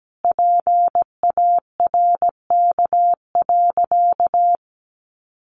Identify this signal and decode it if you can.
Morse code, heard as a keyed tone, about 17 wpm, 700 Hz. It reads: PARK.
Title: Sentences from Top 1300 Words 17wpm